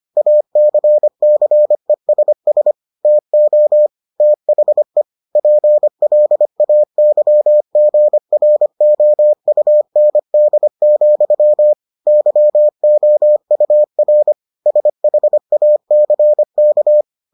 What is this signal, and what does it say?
Morse code, heard as a keyed tone, about 25 wpm, 600 Hz. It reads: ACCESS TO THE PLAYGROUND, YOUR SHACK